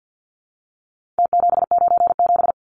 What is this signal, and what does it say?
Morse code, heard as a keyed tone, about 48 wpm, 700 Hz. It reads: T797
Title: Numbers - Contest Style - 4-Digit 50wpm